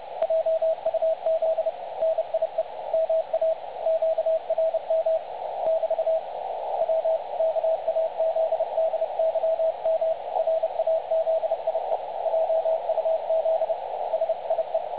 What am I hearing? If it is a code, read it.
Morse code, heard as a keyed tone, about 29 wpm, 660 Hz, with radio noise. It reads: JUZ NIE MA QRM <BT> W KAZDYM RAZIE JUZ SIE
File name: dlazaawansowanych / nagraniazpasma / websdr recording 3565.0kHz